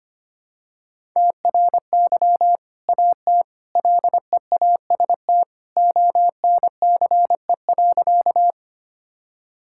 Morse code, heard as a keyed tone, about 25 wpm, 700 Hz. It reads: TRY AT LEAST ONCE.
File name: practice / 25wpm / Sentences with 2-Word Phrases - 25wpm